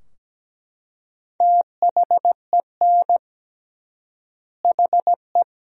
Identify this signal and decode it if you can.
Morse code, heard as a keyed tone, about 17 wpm, 700 Hz. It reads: THEN HE
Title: 3x Word-Spacing - Sentences from Top 500 Words - 17wpm